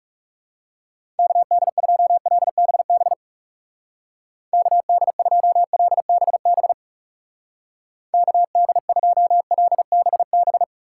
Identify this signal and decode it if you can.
Morse code, heard as a keyed tone, about 36 wpm, 700 Hz. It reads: KD2LBB KD2LBB KD2LBB
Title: Speed Racing - USA 2x3 Call Signs 30wpm